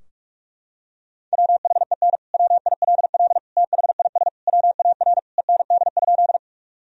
Morse code, heard as a keyed tone, about 46 wpm, 700 Hz. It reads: WHEN WILL THIS WAR END?
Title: Sentences from Top 1000 Words - Review Part 2 - 45wpm